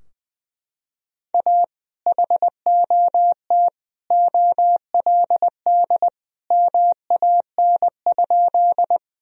Morse code, heard as a keyed tone, about 20 wpm, 700 Hz. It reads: A HOT OLD MAN?
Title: Sentences from Top 1000 Words - Review Part 3 - 20wpm